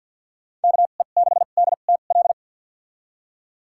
Morse code, heard as a keyed tone, about 53 wpm, spaced as for 37 wpm, 690 Hz. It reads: KE6DTL
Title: USA Call Signs - 2x Character Spacing - 50wpm